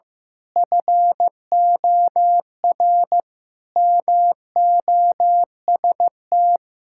Morse code, heard as a keyed tone, about 16 wpm, 700 Hz. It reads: FOR MOST
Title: Pangram Challenge v2 - 15wpm